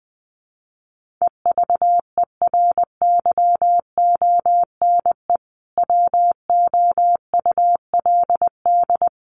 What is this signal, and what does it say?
Morse code, heard as a keyed tone, about 20 wpm, 700 Hz. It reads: EVERYONE WOULD
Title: Sentences from Top 1500 Words 20wpm